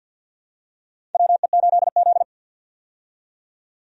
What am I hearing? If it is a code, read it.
Morse code, heard as a keyed tone, about 52 wpm, 700 Hz. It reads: WE8Z
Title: USA Call Signs - 4 Characters - 50wpm